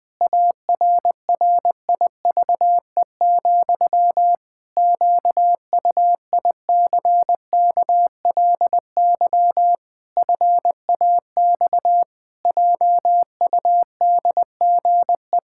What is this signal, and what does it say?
Morse code, heard as a keyed tone, about 20 wpm, 700 Hz. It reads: ARRIVE, QUICKLY FAX JUDGE
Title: Pangram Challenge v2 - 20wpm